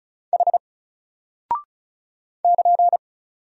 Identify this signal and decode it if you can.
Morse code, heard as a keyed tone, about 36 wpm, 700 Hz. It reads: H <KN>
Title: Single Character (Letter:Number:Punctuation:Prosign) Alt 35wpm